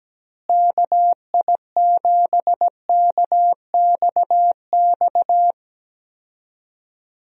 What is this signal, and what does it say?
Morse code, heard as a keyed tone, about 17 wpm, 700 Hz. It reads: KI7KXX
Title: USA Call Signs - 6 Characters - Repeated 3x - 17wpm